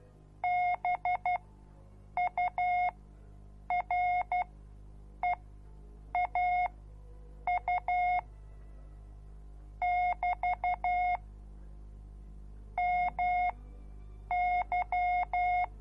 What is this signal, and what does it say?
Morse code, heard as a keyed tone, about 12 wpm, spaced as for 7 wpm, 720 Hz. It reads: BUREAU <BT> MY